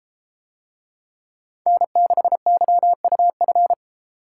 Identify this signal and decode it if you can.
Morse code, heard as a keyed tone, about 32 wpm, 700 Hz. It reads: N6YUF